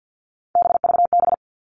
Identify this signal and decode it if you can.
Morse code, heard as a keyed tone, about 50 wpm, 700 Hz. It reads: BVB
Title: All Cut Numbers - 3-Digit 50wpm